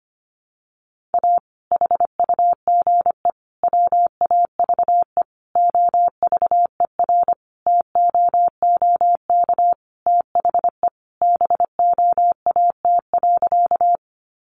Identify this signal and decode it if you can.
Morse code, heard as a keyed tone, about 25 wpm, 700 Hz. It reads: A HUGE WAVE OVERTOOK THE BOAT.
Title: Sentences from Top 1400 Words 25wpm